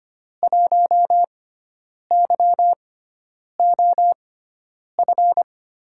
Morse code, heard as a keyed tone, about 25 wpm, spaced as for 9 wpm, 700 Hz. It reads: 1YOF
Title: USA Call Signs - 6 Characters - 6x Character Spacing - 25wpm